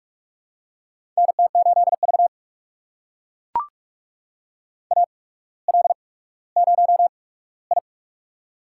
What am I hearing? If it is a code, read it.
Morse code, heard as a keyed tone, about 44 wpm, spaced as for 19 wpm, 700 Hz. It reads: NT8V AL0I
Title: USA Call Signs - 4 Characters - 8x Character Spacing - 45wpm